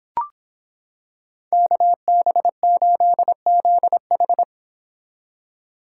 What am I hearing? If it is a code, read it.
Morse code, heard as a keyed tone, about 25 wpm, 700 Hz. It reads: KB8ZH